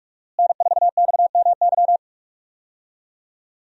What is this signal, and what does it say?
Morse code, heard as a keyed tone, about 43 wpm, 700 Hz. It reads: N4XMY
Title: USA Call Signs - 5 Characters - Repeated 3x - 45wpm